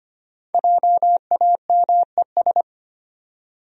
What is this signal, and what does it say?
Morse code, heard as a keyed tone, about 25 wpm, 700 Hz. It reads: JAMES
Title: USA Common Names - Max 5 Letters - 25wpm